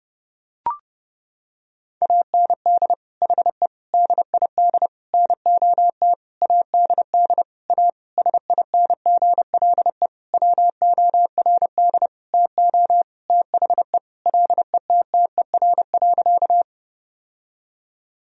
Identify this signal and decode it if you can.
Morse code, heard as a keyed tone, about 32 wpm, 700 Hz. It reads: AND HE DID NOT ADD A SINGLE WORD TO THE LETTER.
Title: Sentences from Top 1000 Words - Review Part 1 - 30wpm